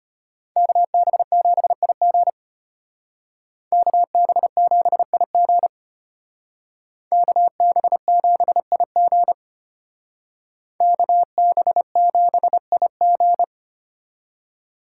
Morse code, heard as a keyed tone, about 32 wpm, 700 Hz. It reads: KB7IG KB7IG KB7IG KB7IG